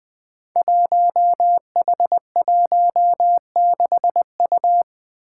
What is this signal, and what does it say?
Morse code, heard as a keyed tone, about 20 wpm, 700 Hz. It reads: 1H16U
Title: Five Challenging Letter-Number Combinations 20wpm